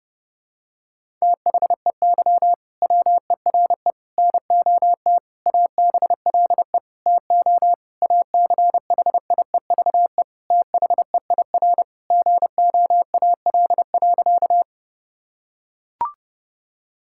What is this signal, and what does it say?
Morse code, heard as a keyed tone, about 30 wpm, 700 Hz. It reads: THEY WERE NOT ABLE TO ACHIEVE THEIR GOAL.
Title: Sentences from Top 1000 Words - Review Part 6 - 30wpm